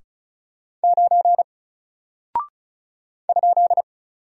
Morse code, heard as a keyed tone, about 36 wpm, 700 Hz. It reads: 9 ?
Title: Single Character (Letter:Number:Punctuation:Prosign) Alt 35wpm